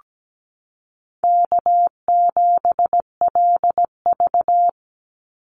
Morse code, heard as a keyed tone, about 17 wpm, 700 Hz. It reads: K7LV
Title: Call Signs Repeated 2x - 17wpm